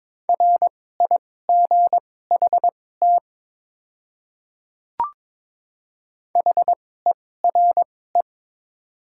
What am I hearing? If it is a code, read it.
Morse code, heard as a keyed tone, about 22 wpm, spaced as for 16 wpm, 700 Hz. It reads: RIGHT HERE
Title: 2x Character Spacing - Top 200 Words - 22wpm